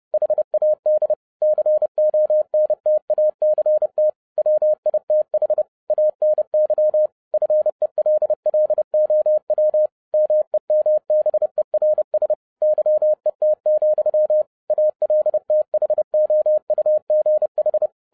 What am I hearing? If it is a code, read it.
Morse code, heard as a keyed tone, about 30 wpm, 610 Hz. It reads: HAD CONTACT WITH ANY FELLOW MEMBERS YET, ALTHOUGH